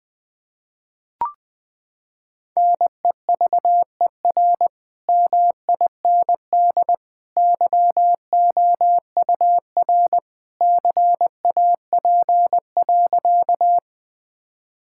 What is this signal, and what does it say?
Morse code, heard as a keyed tone, about 20 wpm, 700 Hz. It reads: NEVER MIND YOUR CAP.
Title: Sentences from Top 1600 Words 20wpm